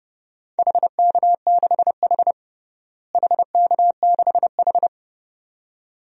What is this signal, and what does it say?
Morse code, heard as a keyed tone, about 32 wpm, 700 Hz. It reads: HK6H HK6H
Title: DXCC-100 Call Signs - 4 Characters - Repeated 2x - 30wpm